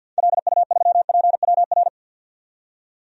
Morse code, heard as a keyed tone, about 51 wpm, 700 Hz. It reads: LU3PWR
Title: DXCC-100 Call Signs - 6 Characters - 50wpm